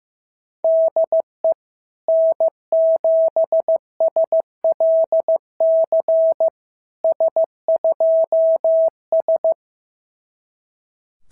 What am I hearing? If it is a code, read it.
Morse code, heard as a keyed tone, about 15 wpm, 660 Hz. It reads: DE N7SLC S2S
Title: SOTA QSO Practice 15wpm